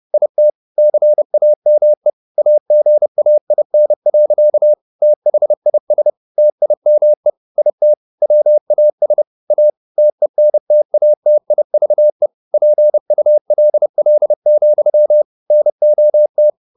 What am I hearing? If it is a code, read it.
Morse code, heard as a keyed tone, about 30 wpm, 600 Hz. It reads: IT CAME AGAIN. THIS TIME IT WAS A TENTATIVE PULL, NOT
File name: the old man and the sea / the old man and the sea 1 30wpm